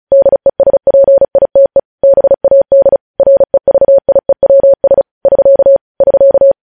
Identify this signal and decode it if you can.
Morse code, heard as a keyed tone, about 36 wpm, 550 Hz. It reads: DESPITE BAD REVIEWS <SK> <SK>
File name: podcast / News headlines-1x-35x35